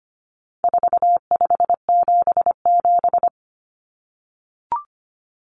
Morse code, heard as a keyed tone, about 25 wpm, 700 Hz. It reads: 4577
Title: Common Cut Numbers - 4-Digit 25wpm